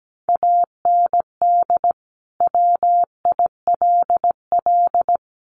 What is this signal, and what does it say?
Morse code, heard as a keyed tone, about 17 wpm, 700 Hz. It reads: AND WILL
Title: Sentences from Top 1200 Words 17wpm